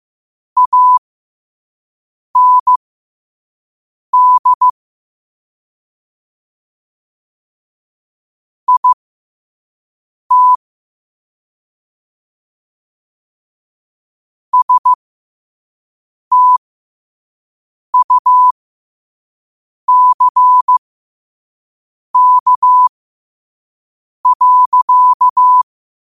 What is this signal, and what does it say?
Morse code, heard as a keyed tone, about 15 wpm, spaced as for 5 wpm, 1000 Hz. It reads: AND IT STUCK.
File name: qotdfiles / 5wpm